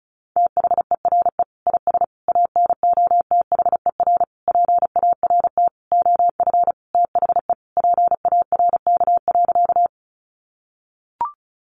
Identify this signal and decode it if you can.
Morse code, heard as a keyed tone, about 34 wpm, 700 Hz. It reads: THERE IS ANOTHER PART OF THE PARK.
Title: Sentences from Top 1300 Words 35wpm